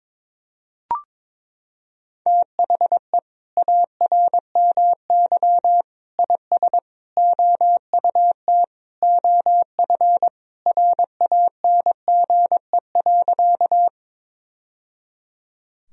Morse code, heard as a keyed tone, about 22 wpm, 700 Hz. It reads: THE ARMY IS OUT OF RANGE.
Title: Sentences from Top 1000 Words - Review Part 4 - 22wpm